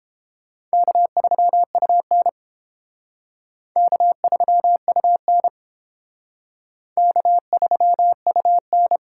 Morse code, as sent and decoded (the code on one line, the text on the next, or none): -.- ...-- ..- -. / -.- ...-- ..- -. / -.- ...-- ..- -.
K3UN K3UN K3UN